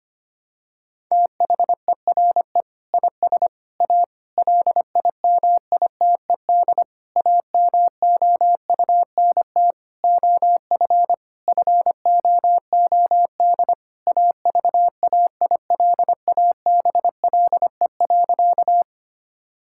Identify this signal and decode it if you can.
Morse code, heard as a keyed tone, about 25 wpm, 700 Hz. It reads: THERE IS A LIMITED AMOUNT OF FOOD AVAILABLE.
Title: Sentences from Top 1000 Words - Review Part 6 - 25wpm